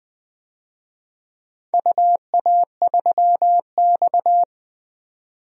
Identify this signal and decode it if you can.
Morse code, heard as a keyed tone, about 20 wpm, 700 Hz. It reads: UA3X
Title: DXCC-100 Call Signs - 4 Characters - 20wpm